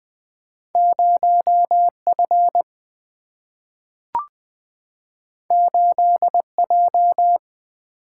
Morse code, heard as a keyed tone, about 20 wpm, 700 Hz. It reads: 0 F 8 J
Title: Single-Letter Single-Number 20wpm